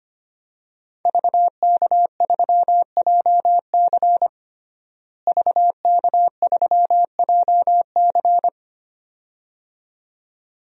Morse code, heard as a keyed tone, about 25 wpm, 700 Hz. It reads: VK3JC VK3JC
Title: DXCC-100 Call Signs - 5 Characters - Repeated 2x - 25wpm